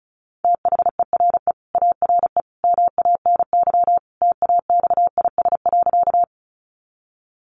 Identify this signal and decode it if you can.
Morse code, heard as a keyed tone, about 34 wpm, 700 Hz. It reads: THERE ARE MANY TAXIS.
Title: Easy Phrases 35wpm